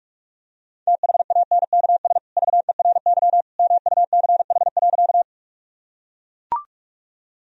Morse code, heard as a keyed tone, about 44 wpm, 700 Hz. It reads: THANKS VERY MUCH.
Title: Sentences from Top 1000 Words - Review Part 3 - 45wpm